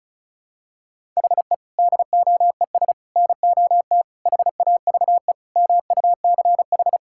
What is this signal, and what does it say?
Morse code, heard as a keyed tone, about 34 wpm, 700 Hz. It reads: HE DOES NOT HAVE MUCH
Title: Sentences from Top 1000 Words - Review Part 5 - 35wpm